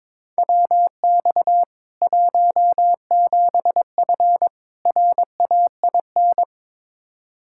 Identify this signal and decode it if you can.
Morse code, heard as a keyed tone, about 21 wpm, 700 Hz. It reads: WX 17F RAIN
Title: K1USN SST QSO Practice 22wpm